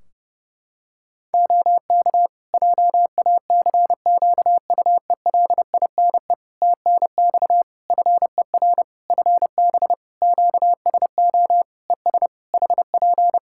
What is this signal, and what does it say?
Morse code, heard as a keyed tone, about 30 wpm, 700 Hz. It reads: OK JACQUELINE TNX FER FB QSO ES HP
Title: K1USN SST QSO Practice 30wpm